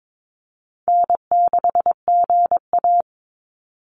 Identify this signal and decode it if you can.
Morse code, heard as a keyed tone, about 22 wpm, 700 Hz. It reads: N6GA
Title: Easy Call Signs Repeated 2x - 22wpm